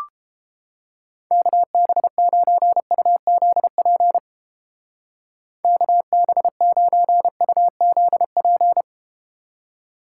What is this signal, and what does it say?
Morse code, heard as a keyed tone, about 33 wpm, 700 Hz. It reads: KB9UZP KB9UZP